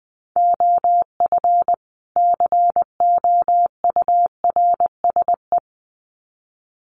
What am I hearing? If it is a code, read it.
Morse code, heard as a keyed tone, about 20 wpm, 700 Hz. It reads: OF COURSE
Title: Sets of 2 Words 20wpm